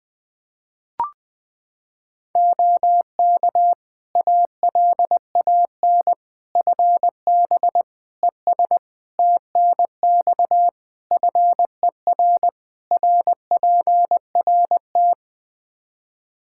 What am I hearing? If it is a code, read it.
Morse code, heard as a keyed tone, about 21 wpm, 700 Hz. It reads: OK ALAN FB ES TNX FER RPRT
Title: Practice QSO - Exchange 2 - Info - 20wpm